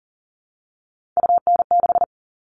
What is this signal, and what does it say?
Morse code, heard as a keyed tone, about 37 wpm, 700 Hz. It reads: UN6
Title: All Cut Numbers - 3-Digit 40wpm